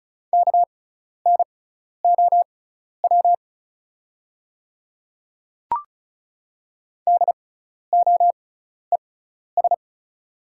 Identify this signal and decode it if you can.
Morse code, heard as a keyed tone, about 34 wpm, spaced as for 12 wpm, 700 Hz. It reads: KNOW DOES
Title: 6x Character Spacing - Top 500 Words - 35wpm